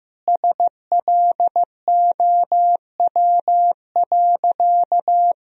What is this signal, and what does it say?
Morse code, heard as a keyed tone, about 15 wpm, 700 Hz. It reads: SLOW.
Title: Sentences from Top 1000 Words - Review Part 2 - 15wpm